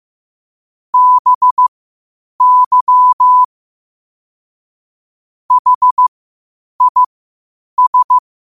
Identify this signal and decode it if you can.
Morse code, heard as a keyed tone, about 15 wpm, spaced as for 8 wpm, 1000 Hz. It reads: BY HIS
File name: qotdfiles / 8wpm